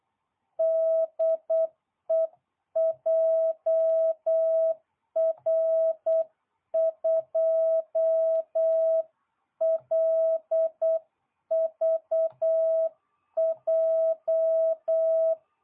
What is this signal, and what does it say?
Morse code, heard as a keyed tone, about 8 wpm, 660 Hz. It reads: DE JR2LVJ